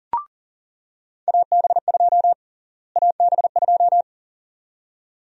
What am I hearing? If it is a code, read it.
Morse code, heard as a keyed tone, about 40 wpm, 700 Hz. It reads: AB2 AB2
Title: USA Call Sign Prefixes - Repeated 2x - 40wpm